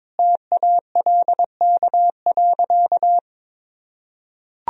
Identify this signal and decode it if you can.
Morse code, heard as a keyed tone, about 22 wpm, 700 Hz. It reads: TALK.
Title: Sentences from Top 1000 Words - Review Part 1 - 22wpm